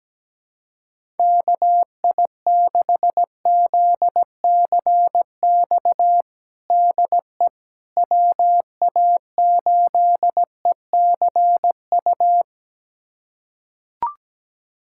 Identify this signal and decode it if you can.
Morse code, heard as a keyed tone, about 17 wpm, 700 Hz. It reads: KI6ZCX DE WA8ECU
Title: USA Callsign DE Callsign - 6 Characters - 17wpm